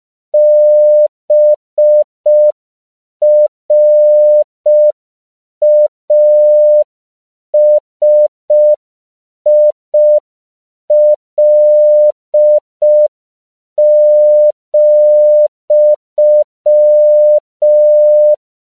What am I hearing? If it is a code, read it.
Morse code, heard as a keyed tone, about 5 wpm, 600 Hz. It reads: BRASIL,
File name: CW / 5PPM